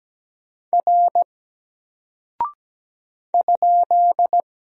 Morse code, heard as a keyed tone, about 17 wpm, 700 Hz. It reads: R ?
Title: Single Character (Letter:Number:Punctuation:Prosign) 17wpm